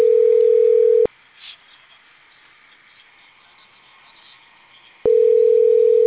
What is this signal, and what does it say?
Ringback tone (North America), 440 + 480 Hz.